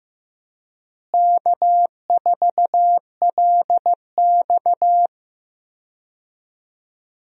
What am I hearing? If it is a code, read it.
Morse code, heard as a keyed tone, about 16 wpm, 700 Hz. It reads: K4LX
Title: USA Call Signs - 4 Characters - Repeated 3x - 15wpm